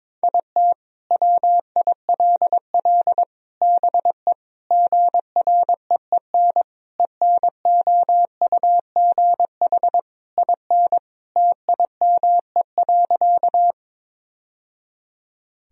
Morse code, heard as a keyed tone, about 22 wpm, 700 Hz. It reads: IT WILL BE GREEN ENOUGH IN TIME.
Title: Sentences from Top 1000 Words - Review Part 2 - 22wpm